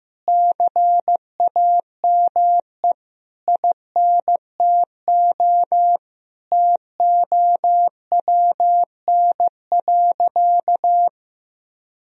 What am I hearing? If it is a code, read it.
Morse code, heard as a keyed tone, about 15 wpm, 700 Hz. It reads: CAME INTO TOWN.
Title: Sentences from Top 1000 Words - Review Part 2 - 15wpm